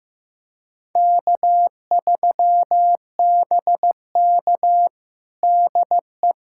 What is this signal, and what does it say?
Morse code, heard as a keyed tone, about 15 wpm, 700 Hz. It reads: K3BK DE
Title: USA Callsign DE Callsign - 4 Characters - 15wpm